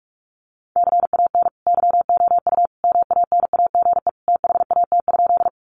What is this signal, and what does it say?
Morse code, heard as a keyed tone, about 44 wpm, 700 Hz. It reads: CAN YOU MANAGE THAT?
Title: Sentences from Top 1400 Words 45wpm